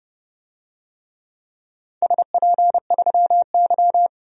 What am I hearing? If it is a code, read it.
Morse code, heard as a keyed tone, about 30 wpm, 700 Hz. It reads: SP3Y
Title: DXCC-100 Call Signs - 4 Characters - 30wpm